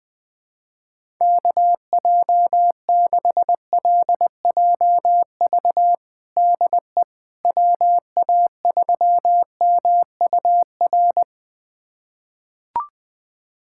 Morse code, heard as a keyed tone, about 20 wpm, 700 Hz. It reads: KJ6LJV DE WA3MUR
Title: USA Callsign DE Callsign - 6 Characters - 20wpm